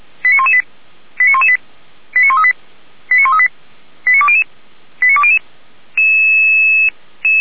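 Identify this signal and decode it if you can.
Five-tone selective call (ZVEI), heard as quick runs of five tones, tones of about 70 ms. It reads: 89118 89118 89138 89138 89200 89200
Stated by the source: ZVEI five-tone call received on SDR.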